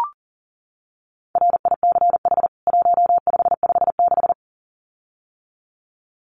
Morse code, heard as a keyed tone, about 40 wpm, 700 Hz. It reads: RICH 1556
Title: CWT Exchange 40wpm